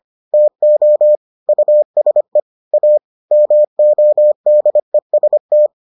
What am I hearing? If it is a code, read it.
Morse code, heard as a keyed tone, about 25 wpm, 600 Hz. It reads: TO USE A MODEST